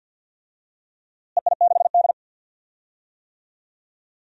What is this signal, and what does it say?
Morse code, heard as a keyed tone, about 47 wpm, 700 Hz. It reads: EI6D
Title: DXCC-100 Call Signs - 4 Characters - Repeated 2x - 50wpm